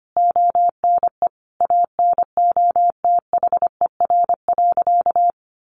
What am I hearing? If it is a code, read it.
Morse code, heard as a keyed tone, about 25 wpm, 700 Hz. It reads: ONE ANOTHER.
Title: Sentences from Top 1400 Words 25wpm